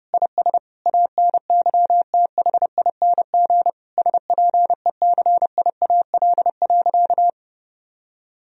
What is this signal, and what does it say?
Morse code, heard as a keyed tone, about 30 wpm, 700 Hz. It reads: IS ANYTHING SPECIAL.
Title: Sentences from Top 1000 Words - Review Part 3 - 30wpm